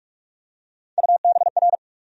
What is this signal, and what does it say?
Morse code, heard as a keyed tone, about 47 wpm, 700 Hz. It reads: UBR
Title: USA Call Sign Postfixes - 3 Letters 45wpm